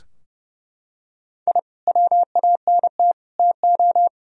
Morse code, heard as a keyed tone, about 31 wpm, 700 Hz. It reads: I WANT TO
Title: Sentences from Top 1000 Words - Review Part 4 - 30wpm